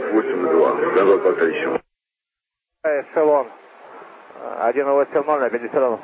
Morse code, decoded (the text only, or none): none